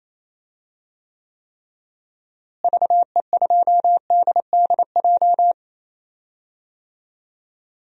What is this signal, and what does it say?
Morse code, heard as a keyed tone, about 28 wpm, 700 Hz. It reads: VE2DDJ
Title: DXCC-100 Call Signs - 6 Characters - Repeated 3x - 28wpm